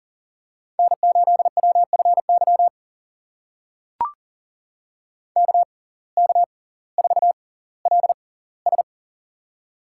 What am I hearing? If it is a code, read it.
Morse code, heard as a keyed tone, about 40 wpm, spaced as for 20 wpm, 700 Hz. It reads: N8WFY KK4LS
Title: USA Call Signs - 5 Characters - 6x Character Spacing - 40wpm